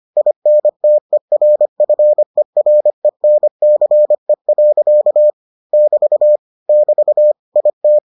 Morse code, heard as a keyed tone, about 25 wpm, 600 Hz. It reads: INTERFERENCE. <BT> <BT> IT